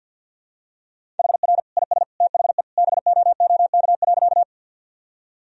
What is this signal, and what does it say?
Morse code, heard as a keyed tone, about 52 wpm, 700 Hz. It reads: HR IS THE BOOK.
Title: Sentences from Top 300 Words and Abbreviations 50wpm